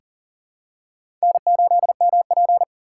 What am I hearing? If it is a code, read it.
Morse code, heard as a keyed tone, about 40 wpm, 690 Hz. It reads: N8MP
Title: USA Call Signs - 4x Character Spacing - 40wpm